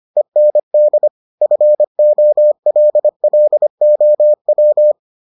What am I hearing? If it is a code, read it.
Morse code, heard as a keyed tone, about 25 wpm, 600 Hz. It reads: END FOLLOW